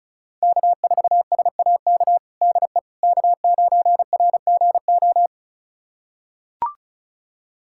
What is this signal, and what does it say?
Morse code, heard as a keyed tone, about 36 wpm, 700 Hz. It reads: K4SAK DE K9RGO
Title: USA Callsign DE Callsign - 5 Characters - 35wpm